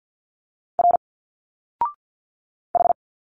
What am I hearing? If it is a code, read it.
Morse code, heard as a keyed tone, about 50 wpm, 700 Hz. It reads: R H
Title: Single Letters 50wpm